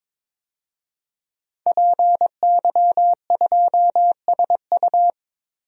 Morse code, heard as a keyed tone, about 22 wpm, 700 Hz. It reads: PY2SU
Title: DXCC-100 Call Signs - 5 Characters - Repeated 2x - 22wpm